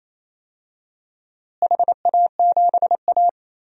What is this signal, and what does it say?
Morse code, heard as a keyed tone, about 28 wpm, 700 Hz. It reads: HA7A
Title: DXCC-100 Call Signs - 4 Characters - Repeated 3x - 28wpm